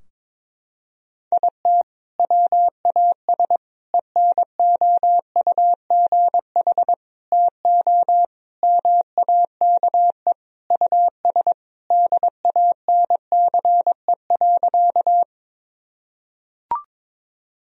Morse code, heard as a keyed tone, about 22 wpm, 700 Hz. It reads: IT WAS ENOUGH TO MAKE US DANCE.
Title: Sentences from Top 1000 Words - Review Part 4 - 22wpm